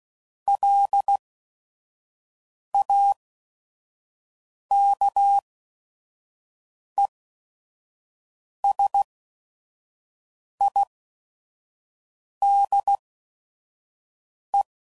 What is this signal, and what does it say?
Morse code, heard as a keyed tone, about 16 wpm, spaced as for 5 wpm, 800 Hz. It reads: LAKESIDE
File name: CW Code Groups @5WPM / fromSM's / Cities